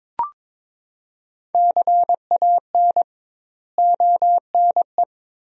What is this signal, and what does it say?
Morse code, heard as a keyed tone, about 21 wpm, 700 Hz. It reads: CAN ONE
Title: 2x Word-Spacing - Sentences from Top 100 Words - 22wpm